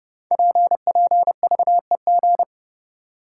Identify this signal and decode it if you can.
Morse code, heard as a keyed tone, about 30 wpm, 700 Hz. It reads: PPVEG
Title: Code Groups - 5 Letter:Numbers - 30wpm